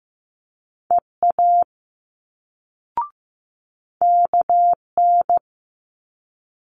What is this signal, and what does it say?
Morse code, heard as a keyed tone, about 15 wpm, 700 Hz. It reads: EA KN
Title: Multi-letter Phonemes Quick Fire 15wpm